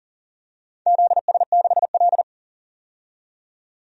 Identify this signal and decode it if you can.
Morse code, heard as a keyed tone, about 40 wpm, 700 Hz. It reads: ZS6L
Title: DXCC-100 Call Signs - 4 Characters - 40wpm